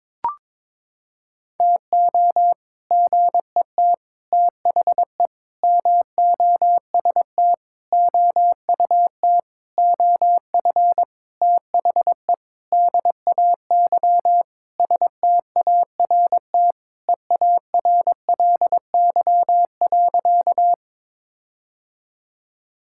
Morse code, heard as a keyed tone, about 22 wpm, 700 Hz. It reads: TO GET THE MOST OUT OF THE DAY START EARLY.
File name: Sentences with 7-Word Phrases - 22wpm